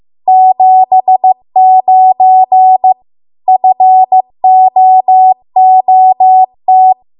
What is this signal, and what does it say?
Morse code, heard as a keyed tone, about 15 wpm, 750 Hz. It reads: 79 FOOT